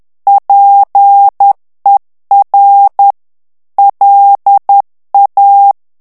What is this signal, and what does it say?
Morse code, heard as a keyed tone, about 10 wpm, 800 Hz. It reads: PER LA
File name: morse mp3 / ITALIEN texte clair 1 vitesse 10 mots minute